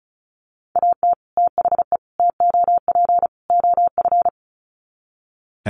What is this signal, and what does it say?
Morse code, heard as a keyed tone, about 34 wpm, 700 Hz. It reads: AT THE TOP OF
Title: Sets of 4 Words 35wpm